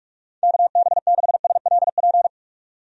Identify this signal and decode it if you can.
Morse code, heard as a keyed tone, about 46 wpm, 700 Hz. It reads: KB6SLP
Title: USA Call Signs - 6 Characters - 45wpm